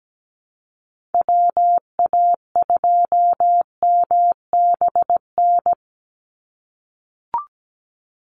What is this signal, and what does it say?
Morse code, heard as a keyed tone, about 17 wpm, 700 Hz. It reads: WA2MBN
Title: USA Call Signs 17wpm